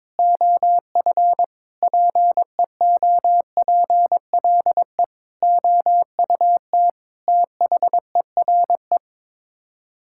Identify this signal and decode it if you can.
Morse code, heard as a keyed tone, about 22 wpm, 700 Hz. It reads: OF PEOPLE OUT THERE
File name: practice / 22wpm / Sets of 8 Words - 22wpm